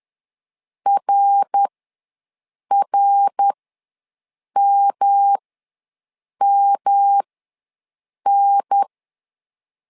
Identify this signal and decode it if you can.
Morse code, heard as a keyed tone, about 10 wpm, spaced as for 6 wpm, 800 Hz. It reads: RRMMN
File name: LEZIONI / 3 MWRO / 3 MWR ws 10wpm 2151